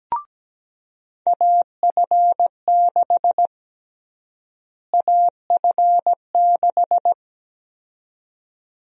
Morse code, heard as a keyed tone, about 17 wpm, 700 Hz. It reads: AF6 AF6
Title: USA Call Sign Prefixes - Repeated 2x - 17wpm